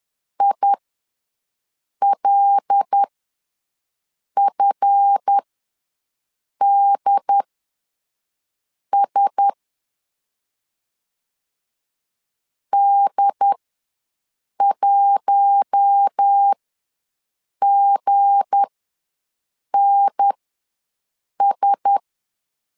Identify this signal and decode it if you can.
Morse code, heard as a keyed tone, about 10 wpm, spaced as for 5 wpm, 800 Hz. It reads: ILFDS D1GNS
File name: LEZIONI / 7 GF9Z / 7 GF ws 10wpm 3918